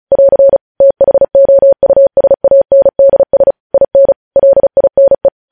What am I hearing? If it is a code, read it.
Morse code, heard as a keyed tone, about 36 wpm, 550 Hz. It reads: <AR> THOUSANDS IN LINE
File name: podcast / News headlines-1x-35x35